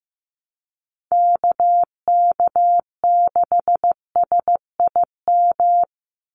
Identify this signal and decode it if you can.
Morse code, heard as a keyed tone, about 15 wpm, 700 Hz. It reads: KK6SIM
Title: USA Call Signs 15wpm